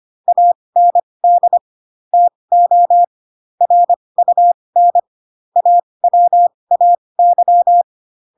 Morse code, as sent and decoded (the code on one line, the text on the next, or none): .- -. -.. / - --- / .-. ..- -. / .- .-- .- -.--
AND TO RUN AWAY